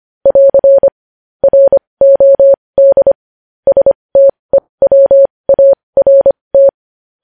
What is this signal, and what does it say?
Morse code, heard as a keyed tone, about 25 wpm, spaced as for 20 wpm, 550 Hz. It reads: <AR> ROD STEWART
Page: News headlines in Morse code at 20 wpm